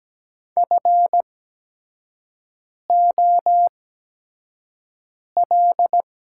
Morse code, heard as a keyed tone, about 17 wpm, spaced as for 5 wpm, 700 Hz. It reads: FOL
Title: 8x Character Spacing - Top 300 Words - 17wpm